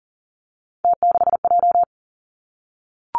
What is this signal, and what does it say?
Morse code, heard as a keyed tone, about 40 wpm, 700 Hz. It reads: T6J
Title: Three Challenging Letter-Number Combinations 40wpm